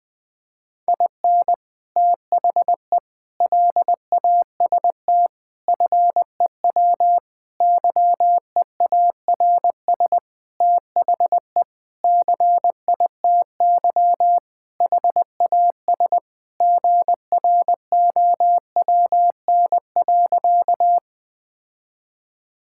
Morse code, heard as a keyed tone, about 20 wpm, 700 Hz. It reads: IN THE LAST FEW YEARS THE CITY HAS GROWN.
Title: Sentences with 5-Word Phrases - 20wpm